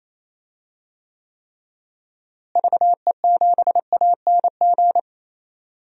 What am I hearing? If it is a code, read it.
Morse code, heard as a keyed tone, about 28 wpm, 700 Hz. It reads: VE7ANG
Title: DXCC-100 Call Signs - 6 Characters - Repeated 3x - 28wpm